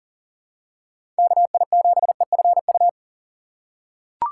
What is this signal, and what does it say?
Morse code, heard as a keyed tone, about 40 wpm, 690 Hz. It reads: KI7EFU
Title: USA Call Signs - 6x Character Spacing - 40wpm